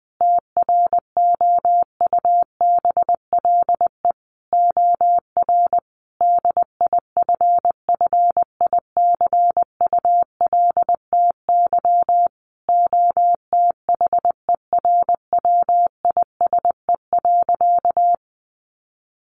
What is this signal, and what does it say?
Morse code, heard as a keyed tone, about 20 wpm, 700 Hz. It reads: TROUBLE OR DIFFICULTY OTHERWISE.
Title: Sentences from Top 1500 Words 20wpm